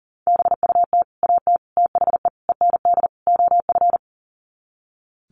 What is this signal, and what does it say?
Morse code, heard as a keyed tone, about 40 wpm, 700 Hz. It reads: BUT AT THE END OF
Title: Sets of 5 Words Encore2 40wpm